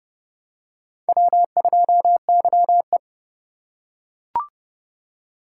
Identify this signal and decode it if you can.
Morse code, heard as a keyed tone, about 29 wpm, 700 Hz. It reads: W2YE